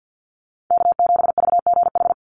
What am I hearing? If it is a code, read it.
Morse code, heard as a keyed tone, about 50 wpm, 700 Hz. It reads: K7VGH
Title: USA Call Signs 50wpm